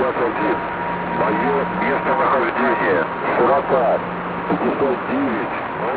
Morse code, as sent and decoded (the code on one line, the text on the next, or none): none